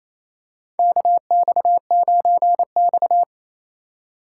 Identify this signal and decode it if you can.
Morse code, heard as a keyed tone, about 28 wpm, 700 Hz. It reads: KX9X
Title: USA Call Signs - 4 Characters - 3x Character Spacing - 28wpm